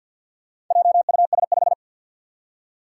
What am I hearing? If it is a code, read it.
Morse code, heard as a keyed tone, about 51 wpm, 700 Hz. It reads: JUS5
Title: Four Challenging Letter-Number Combinations 50wpm